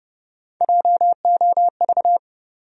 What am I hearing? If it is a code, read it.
Morse code, heard as a keyed tone, about 30 wpm, 700 Hz. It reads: JOV
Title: USA Call Sign Postfixes - 3 Letters 30wpm